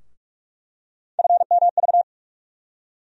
Morse code, heard as a keyed tone, about 45 wpm, 700 Hz. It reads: FMV
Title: USA Call Sign Postfixes - 3 Letters 45wpm